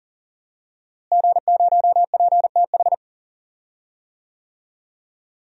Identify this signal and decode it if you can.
Morse code, heard as a keyed tone, about 40 wpm, 700 Hz. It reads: G0PTH
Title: DXCC-100 Call Signs - 5 Characters - Repeated 2x - 40wpm